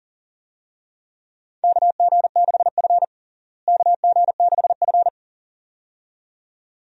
Morse code, heard as a keyed tone, about 40 wpm, 700 Hz. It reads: KG6F KG6F
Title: USA Call Signs - 4 Characters - Repeated 2x - 40wpm